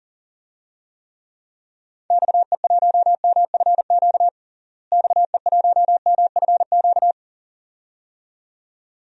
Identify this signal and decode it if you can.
Morse code, heard as a keyed tone, about 40 wpm, 700 Hz. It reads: XE1MFQ XE1MFQ
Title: DXCC-100 Call Signs - 6 Characters - Repeated 2x - 40wpm